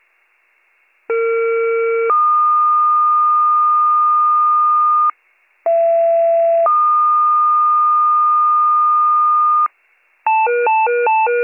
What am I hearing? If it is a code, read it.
Two-tone page, heard as a one-second tone, then a three-second tone, about 460 then 1220 Hz, about 670 then 1220 Hz. It is heard from 1.1 to 9.7 s.